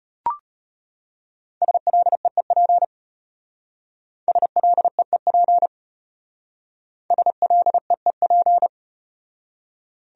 Morse code, heard as a keyed tone, about 38 wpm, 700 Hz. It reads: SLEEP SLEEP SLEEP